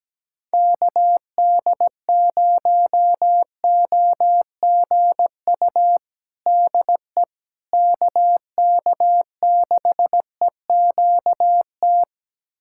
Morse code, heard as a keyed tone, about 17 wpm, 700 Hz. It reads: KD0OGU DE KK6EQT
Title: USA Callsign DE Callsign - 6 Characters - 17wpm